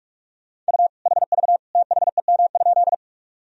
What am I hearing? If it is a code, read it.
Morse code, heard as a keyed tone, about 46 wpm, 700 Hz. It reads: U HV THEM?
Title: Sentences from Top 100 Words and Abbreviations 45wpm